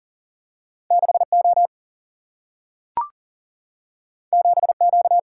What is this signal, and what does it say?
Morse code, heard as a keyed tone, about 40 wpm, 700 Hz. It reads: 6 O 7 Q
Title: Single-Letter Single-Number 40wpm